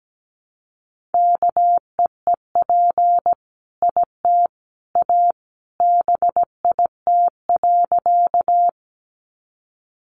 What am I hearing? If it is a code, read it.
Morse code, heard as a keyed tone, about 17 wpm, 700 Hz. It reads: KEEP IT A BIT.
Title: Sentences from Top 900 Words 17wpm